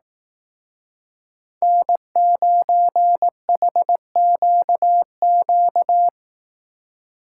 Morse code, heard as a keyed tone, about 18 wpm, 700 Hz. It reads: N9HQQ
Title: Speed Racing - USA 1x3 Call Signs 15wpm